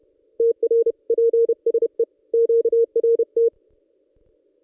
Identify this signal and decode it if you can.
Morse code, heard as a keyed tone, about 31 wpm, 460 Hz. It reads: TR PSE QRT
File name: contests / ss-cw-k5tr-op-n5rz -1600